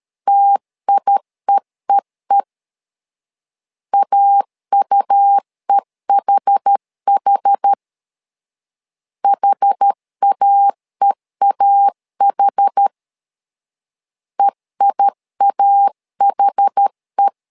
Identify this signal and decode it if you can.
Morse code, heard as a keyed tone, about 12 wpm, 800 Hz. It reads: TIEEE AUEHH HAEAH EIAHE
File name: LEZIONI / 2 UHDN / 2 UH 12wpm 1439